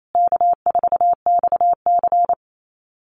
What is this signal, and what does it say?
Morse code, heard as a keyed tone, about 28 wpm, 700 Hz. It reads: K4XC
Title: USA Call Signs Repeated 3x 28wpm